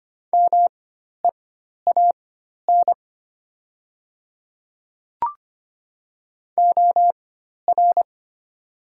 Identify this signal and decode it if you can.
Morse code, heard as a keyed tone, about 25 wpm, spaced as for 12 wpm, 700 Hz. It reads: MEAN OR
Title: 4x Character Spacing - Top 200 Words - 25wpm